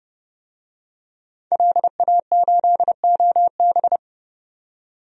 Morse code, heard as a keyed tone, about 30 wpm, 700 Hz. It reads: LA8OB
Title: DXCC-100 Call Signs - 5 Characters - 30wpm